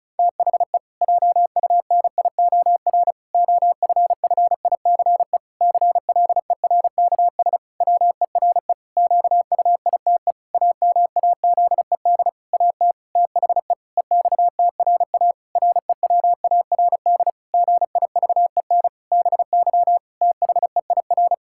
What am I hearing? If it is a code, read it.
Morse code, heard as a keyed tone, about 36 wpm, 700 Hz. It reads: THE JUNIOR OFFICE CLERKS WERE QUITE AMAZED AT THE EXTRA REWARD GIVEN BY THEIR
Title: Pangram Challenge v2 - 35wpm